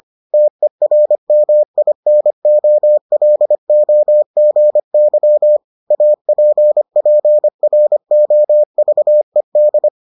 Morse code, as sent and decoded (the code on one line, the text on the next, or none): - . .-. -- .. -. --- .-.. --- --. -.-- / .- .--. .--. .-. --- ...- . -..
TERMINOLOGY APPROVED